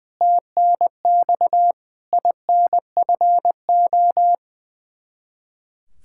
Morse code, heard as a keyed tone, about 20 wpm, 700 Hz. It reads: TNX INFO
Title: Practice QSO - Exchange 3 - More Info - 20wpm